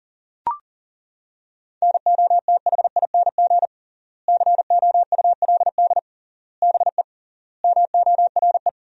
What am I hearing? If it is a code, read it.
Morse code, heard as a keyed tone, about 40 wpm, 700 Hz. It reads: NOTHING COULD BE MORE
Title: 3x Word-Spacing - Sentences from Top 500 Words - 40wpm